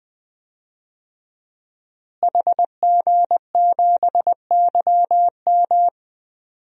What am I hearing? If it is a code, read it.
Morse code, heard as a keyed tone, about 20 wpm, 700 Hz. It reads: HG7YM
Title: DXCC-100 Call Signs - 5 Characters - Repeated 2x - 20wpm